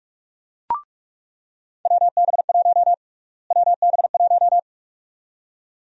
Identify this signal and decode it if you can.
Morse code, heard as a keyed tone, about 41 wpm, 700 Hz. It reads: WB1 WB1
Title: USA Call Sign Prefixes - Repeated 2x - 45wpm